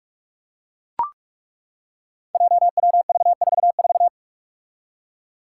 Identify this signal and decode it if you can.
Morse code, heard as a keyed tone, about 41 wpm, 700 Hz. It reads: JWV44
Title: Five Challenging Letter-Number Combinations 45wpm